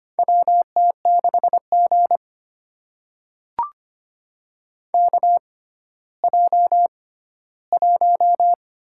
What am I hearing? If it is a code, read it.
Morse code, heard as a keyed tone, about 25 wpm, spaced as for 14 wpm, 700 Hz. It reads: WT6G KJ1
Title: USA Call Signs - 4 Characters - 6x Character Spacing - 25wpm